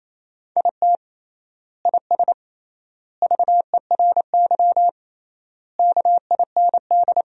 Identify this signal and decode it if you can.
Morse code, heard as a keyed tone, about 28 wpm, 700 Hz. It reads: IT IS VERY KIND
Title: 3x Word-Spacing - Sentences from Top 200 Words - 28wpm